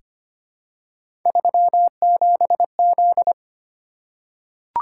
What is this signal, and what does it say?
Morse code, heard as a keyed tone, about 25 wpm, 700 Hz. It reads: 37Z
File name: Three Letter-Number Combinations - 25wpm